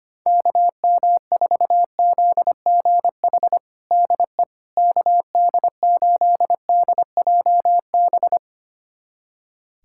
Morse code, heard as a keyed tone, about 25 wpm, 700 Hz. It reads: KM4ZGH DE KD8DJB
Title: USA Callsign DE Callsign 25wpm